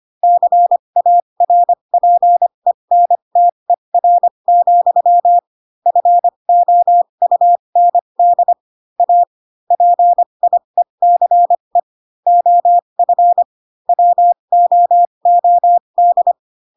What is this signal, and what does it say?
Morse code, heard as a keyed tone, about 25 wpm, spaced as for 22 wpm, 700 Hz. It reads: CARPENTER, FOUND A PIECE OF WOOD